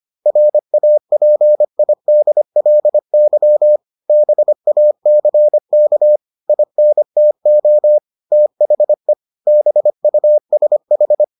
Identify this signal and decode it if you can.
Morse code, heard as a keyed tone, about 25 wpm, 600 Hz. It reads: RAPIDLY BACK INTO THE BUSH